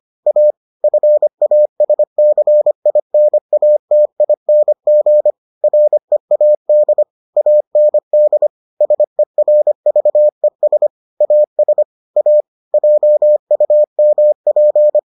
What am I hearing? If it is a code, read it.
Morse code, heard as a keyed tone, about 25 wpm, 600 Hz. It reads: A FASCINATING READ AND SERVES AS A JUMP